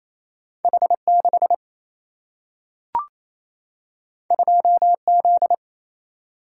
Morse code, heard as a keyed tone, about 28 wpm, 700 Hz. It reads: H6 2Z
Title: Single-Letter Single-Number 28wpm